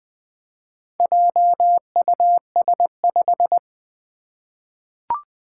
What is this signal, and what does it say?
Morse code, heard as a keyed tone, about 20 wpm, 700 Hz. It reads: JUS5
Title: Four Challenging Letter-Number Combinations 20wpm